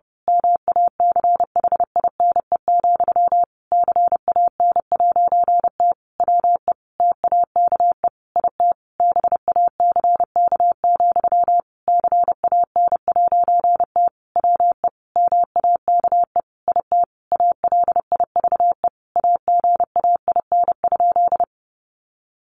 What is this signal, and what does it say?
Morse code, heard as a keyed tone, about 30 wpm, 700 Hz. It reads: MACHINE, CAN'T WE TAKE IT BACK, CAN'T WE MAKE IT ALIVE AGAIN?
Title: A Sound Of Thunder 30wpm